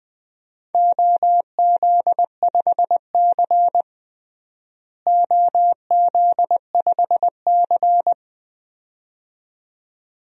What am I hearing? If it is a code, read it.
Morse code, heard as a keyed tone, about 20 wpm, 700 Hz. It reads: OZ5C OZ5C
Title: DXCC-100 Call Signs - 4 Characters - Repeated 2x - 20wpm